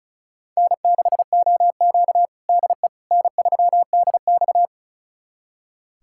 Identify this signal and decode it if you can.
Morse code, heard as a keyed tone, about 36 wpm, 700 Hz. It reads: N6OQ DE N3DX
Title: USA Callsign DE Callsign - 4 Characters - 35wpm